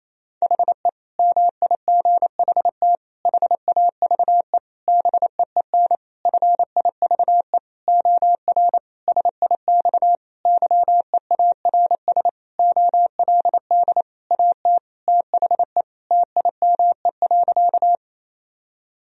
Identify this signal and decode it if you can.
Morse code, heard as a keyed tone, about 28 wpm, 700 Hz. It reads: HE MIGHT HAVE BEEN FIVE OR SIX YEARS OLD AT THE TIME.
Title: Sentences from Top 1000 Words - Review Part 2 - 28wpm